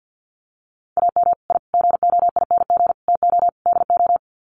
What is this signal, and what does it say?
Morse code, heard as a keyed tone, about 50 wpm, 700 Hz. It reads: AM I GOING TO DO
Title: Sets of 5 Words Encore 50wpm